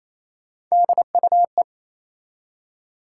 Morse code, heard as a keyed tone, about 28 wpm, 700 Hz. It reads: DUE
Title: Three Letter Words - 28wpm